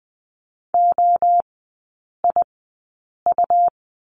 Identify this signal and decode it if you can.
Morse code, heard as a keyed tone, about 20 wpm, spaced as for 8 wpm, 690 Hz. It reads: OIU
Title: 145 Lesson 24e - Letter y - Three Characters 20wpm